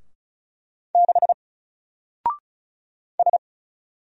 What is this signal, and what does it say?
Morse code, heard as a keyed tone, about 35 wpm, 700 Hz. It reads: 6 S
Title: Single Letter-Number 35wpm